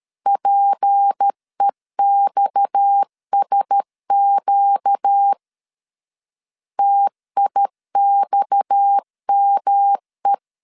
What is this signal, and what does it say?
Morse code, heard as a keyed tone, about 13 wpm, 800 Hz. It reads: PEXSQ TIXME
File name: LEZIONI / 6 PKX0 / 6 PKX 12wpm 2024